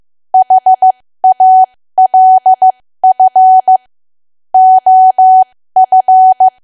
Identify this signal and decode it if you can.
Morse code, heard as a keyed tone, about 15 wpm, spaced as for 13 wpm, 750 Hz. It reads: HALF OF
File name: cwa / W1AW-Level2 / 13WPM